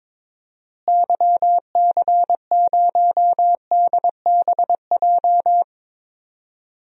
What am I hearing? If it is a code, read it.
Morse code, heard as a keyed tone, about 22 wpm, 700 Hz. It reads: YC0DBJ
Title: DXCC-100 Call Signs - 6 Characters - Repeated 2x - 22wpm